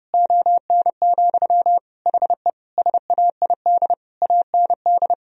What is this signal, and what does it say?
Morse code, heard as a keyed tone, about 30 wpm, 700 Hz. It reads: ON, HE SAID AND
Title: Sentences from Top 1000 Words - Review Part 2 - 30wpm